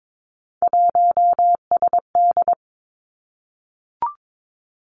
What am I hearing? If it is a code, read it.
Morse code, heard as a keyed tone, about 22 wpm, 700 Hz. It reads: 1SD
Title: Three Challenging Letter-Number Combinations 22wpm